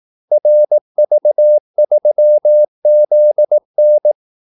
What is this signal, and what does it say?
Morse code, heard as a keyed tone, about 18 wpm, 600 Hz. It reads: RV3ZN
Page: Kaldesignaler 18 wpm